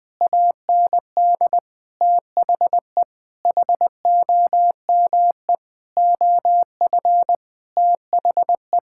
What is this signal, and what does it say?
Morse code, heard as a keyed tone, about 19 wpm, 700 Hz. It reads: AND THE HOME OF THE
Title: Sentences with 8-Word Phrases - 20wpm